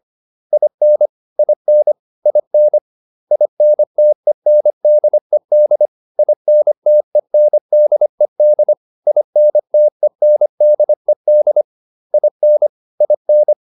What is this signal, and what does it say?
Morse code, heard as a keyed tone, about 25 wpm, 600 Hz. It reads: IN IN IN INTENDED INTENDED INTENDED IN IN